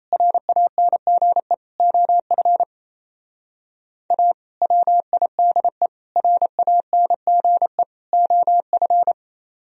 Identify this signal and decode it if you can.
Morse code, heard as a keyed tone, about 34 wpm, 700 Hz. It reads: RANGE OF A WIDE RANGE OF
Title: Speed-Racing - Sets of 4 Words 28wpm